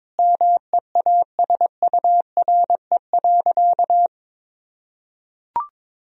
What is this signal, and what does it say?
Morse code, heard as a keyed tone, about 22 wpm, 700 Hz. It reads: MEASURE.
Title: Sentences from Top 1000 Words - Review Part 2 - 22wpm